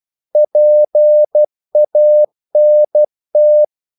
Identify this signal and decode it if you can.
Morse code, heard as a keyed tone, about 12 wpm, 600 Hz. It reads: PANT
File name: Korte ord DK 12wpm